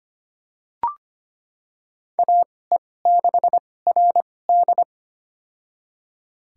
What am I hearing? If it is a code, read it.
Morse code, heard as a keyed tone, about 25 wpm, spaced as for 18 wpm, 700 Hz. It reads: AE6RD
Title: USA Call Signs - 5 Characters - 2x Character Spacing - 25wpm